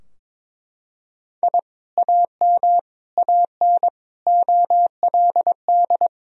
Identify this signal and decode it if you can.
Morse code, heard as a keyed tone, about 22 wpm, 700 Hz. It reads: I AM AN OLD
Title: Sentences from Top 1000 Words - Review Part 2 - 22wpm